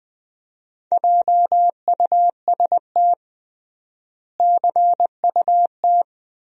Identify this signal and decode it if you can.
Morse code, heard as a keyed tone, about 20 wpm, 700 Hz. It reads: JUST CUT
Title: 3x Word-Spacing - Sentences from Top 300 Words - 20wpm